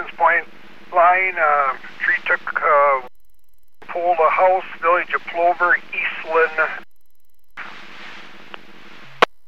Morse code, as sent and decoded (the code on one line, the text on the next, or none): none